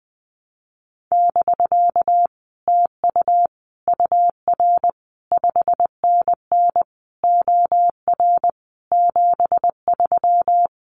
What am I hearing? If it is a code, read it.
Morse code, heard as a keyed tone, about 20 wpm, 700 Hz. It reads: <BK> TU UR 5NN OR 73
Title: Common QSO Phrases Encore 20wpm